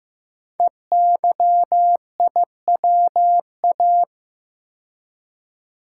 Morse code, heard as a keyed tone, about 15 wpm, 700 Hz. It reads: EYIWA
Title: Code Groups - 5 Letters - 15wpm